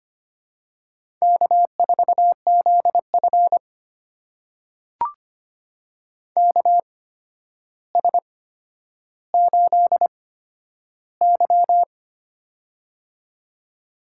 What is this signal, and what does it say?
Morse code, heard as a keyed tone, about 25 wpm, spaced as for 11 wpm, 700 Hz. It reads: K4ZF KS8Y
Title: USA Call Signs - 4 Characters - 8x Character Spacing - 25wpm